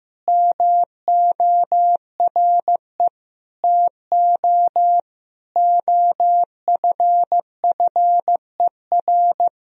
Morse code, heard as a keyed tone, about 15 wpm, 700 Hz. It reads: MORE TO OFFER
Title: Sentences from Top 1000 Words - Review Part 4 - 15wpm